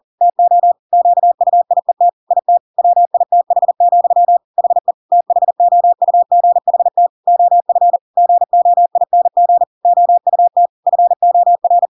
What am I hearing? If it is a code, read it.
Morse code, heard as a keyed tone, about 40 wpm, 700 Hz. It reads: TO QUIET IT WITH, HE THOUGHT OF GOING OUT FOR
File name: Pinocchio / Pinocchio ch5 40 40 ik1hge